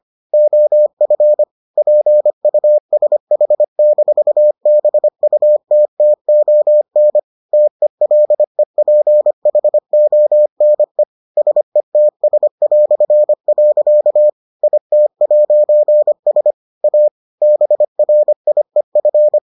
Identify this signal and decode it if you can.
Morse code, heard as a keyed tone, about 25 wpm, 600 Hz. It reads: OF PUSH-BUTTON TELEPHONE SETS". IT'S A BRIEF